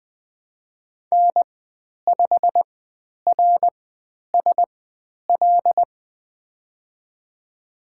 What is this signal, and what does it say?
Morse code, heard as a keyed tone, about 20 wpm, spaced as for 10 wpm, 700 Hz. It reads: N5RSL
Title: 102 Lesson 17g - Letter u - Call Signs 20wpm